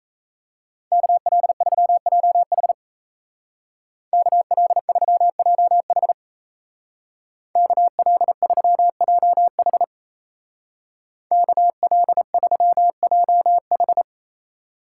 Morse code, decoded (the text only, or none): KL3JH KL3JH KL3JH KL3JH